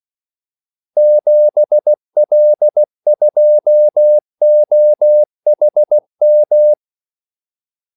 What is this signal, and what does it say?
Morse code, heard as a keyed tone, about 16 wpm, 600 Hz. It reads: 7L2OHM
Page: Kaldesignaler 16 wpm